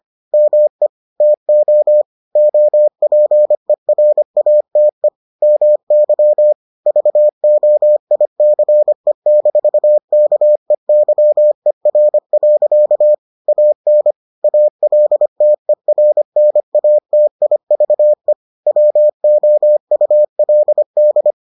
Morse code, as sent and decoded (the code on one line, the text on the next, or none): -- . / - --- / --- .--. .-. .- - . / -- -.-- / ...- --- .. -.-. -....- -.- . -.-- . .-. .-.-.- / .- -. / .- .-.. - . .-. -. .- - .. ...- . / .-- --- ..- .-.. -..
ME TO OPERATE MY VOICE-KEYER. AN ALTERNATIVE WOULD